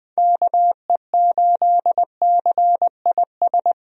Morse code, heard as a keyed tone, about 20 wpm, 700 Hz. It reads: KE8CIS
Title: USA Call Signs - 6 Characters - 20wpm